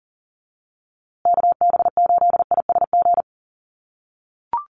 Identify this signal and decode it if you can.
Morse code, heard as a keyed tone, about 40 wpm, 700 Hz. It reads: KB8ISG
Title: USA Call Signs Repeated 2x 40wpm